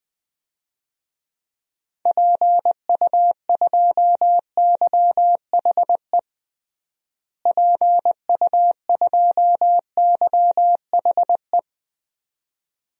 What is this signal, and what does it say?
Morse code, heard as a keyed tone, about 20 wpm, 700 Hz. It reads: PU2YHE PU2YHE
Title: DXCC-100 Call Signs - 6 Characters - Repeated 2x - 20wpm